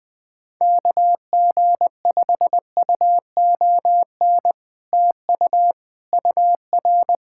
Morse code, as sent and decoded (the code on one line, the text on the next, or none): -.- --. ..... ..- --- -. / - ..- / ..- .-.
KG5UON TU UR